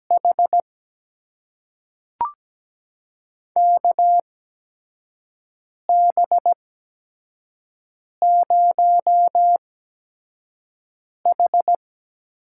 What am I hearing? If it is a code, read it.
Morse code, heard as a keyed tone, about 17 wpm, spaced as for 5 wpm, 700 Hz. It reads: H KB0H
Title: USA Call Signs - 4 Characters - 8x Character Spacing - 17wpm